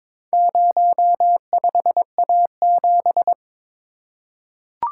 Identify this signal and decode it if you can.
Morse code, heard as a keyed tone, about 22 wpm, 700 Hz. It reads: 05A7
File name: practice / 22wpm / Four Letter-Number Combinations - 22wpm